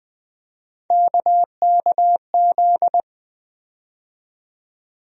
Morse code, heard as a keyed tone, about 21 wpm, 700 Hz. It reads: KKZ
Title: USA Call Sign Postfixes - 3 Letters 20wpm